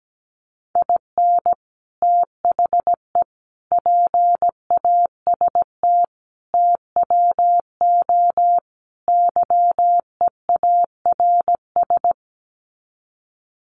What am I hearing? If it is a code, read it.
Morse code, heard as a keyed tone, about 17 wpm, 700 Hz. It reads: IN THE PAST TWO YEARS
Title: Sets of 5 Words 17wpm